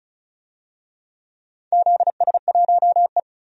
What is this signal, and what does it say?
Morse code, heard as a keyed tone, about 34 wpm, 700 Hz. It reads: ZS1E
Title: DXCC-100 Call Signs - 4 Characters - 35wpm